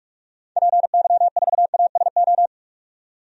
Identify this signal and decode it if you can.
Morse code, heard as a keyed tone, about 46 wpm, 700 Hz. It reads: PY4ASO
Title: DXCC-100 Call Signs - 6 Characters - 45wpm